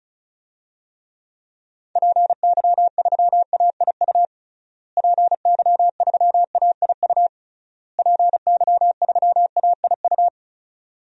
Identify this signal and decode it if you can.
Morse code, heard as a keyed tone, about 36 wpm, 700 Hz. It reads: PY3AIU PY3AIU PY3AIU
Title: DXCC-100 Call Signs - 6 Characters - Repeated 3x - 35wpm